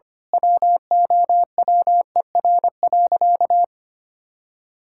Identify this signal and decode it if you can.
Morse code, heard as a keyed tone, about 25 wpm, 700 Hz. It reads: WOWER.
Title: Pangram Challenge v2 - 25wpm